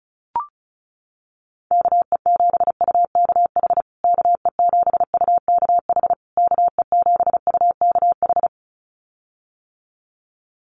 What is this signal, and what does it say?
Morse code, heard as a keyed tone, about 34 wpm, 700 Hz. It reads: KE7UKH KE7UKH KE7UKH
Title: USA Call Signs Repeated 3x 35wpm